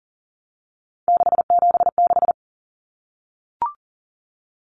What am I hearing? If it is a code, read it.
Morse code, heard as a keyed tone, about 40 wpm, 700 Hz. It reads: 676
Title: Numbers - 3-Digit 40wpm